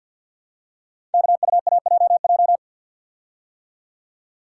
Morse code, heard as a keyed tone, about 50 wpm, 700 Hz. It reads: KUAJJ
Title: Code Groups - 5 Letters - 50wpm